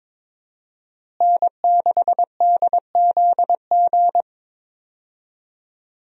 Morse code, heard as a keyed tone, about 22 wpm, 700 Hz. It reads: N6DZG
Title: USA Call Signs - 5 Characters - Repeated 3x - 22wpm